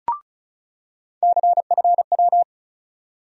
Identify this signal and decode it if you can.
Morse code, heard as a keyed tone, about 36 wpm, 700 Hz. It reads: CFW
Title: USA Call Sign Postfixes - 3 Letters 35wpm